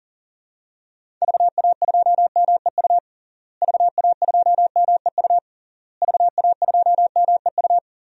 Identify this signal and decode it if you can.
Morse code, heard as a keyed tone, about 40 wpm, 700 Hz. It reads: VA2MEU VA2MEU VA2MEU
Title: DXCC-100 Call Signs - 6 Characters - Repeated 3x - 40wpm